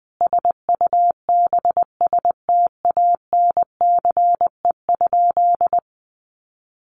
Morse code, heard as a keyed tone, about 20 wpm, 700 Hz. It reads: SUBSTANCE?
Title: Sentences from Top 1500 Words 20wpm